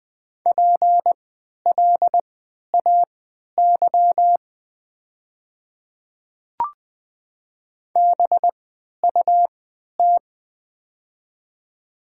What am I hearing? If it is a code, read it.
Morse code, heard as a keyed tone, about 20 wpm, spaced as for 11 wpm, 700 Hz. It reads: PLAY BUT
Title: 3x Character Spacing - Top 500 Words - 20wpm